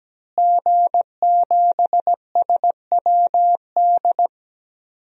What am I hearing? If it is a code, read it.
Morse code, heard as a keyed tone, about 17 wpm, 700 Hz. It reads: G7SWD
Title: DXCC-100 Call Signs - 5 Characters - Repeated 2x - 17wpm